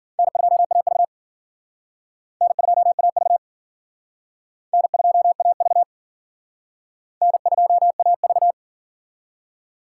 Morse code, heard as a keyed tone, about 52 wpm, 700 Hz. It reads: N2AV N2AV N2AV N2AV